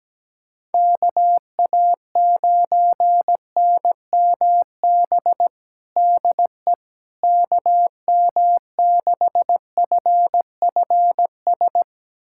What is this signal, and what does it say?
Morse code, heard as a keyed tone, about 17 wpm, 700 Hz. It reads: KA9NMB DE KM6FFS
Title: USA Callsign DE Callsign 17wpm